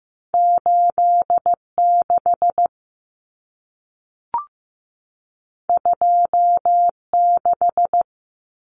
Morse code, heard as a keyed tone, about 15 wpm, 700 Hz. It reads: 8 6 2 6
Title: Numbers - 2-Digit 15wpm